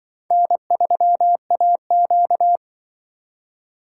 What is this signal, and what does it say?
Morse code, heard as a keyed tone, about 24 wpm, 700 Hz. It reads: N3AQ
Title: Speed Racing - USA Call Signs 20wpm